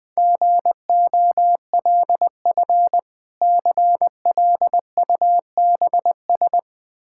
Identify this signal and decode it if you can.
Morse code, heard as a keyed tone, about 20 wpm, 700 Hz. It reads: GOLF CLUBS
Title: Difficult Words with Hints - 20wpm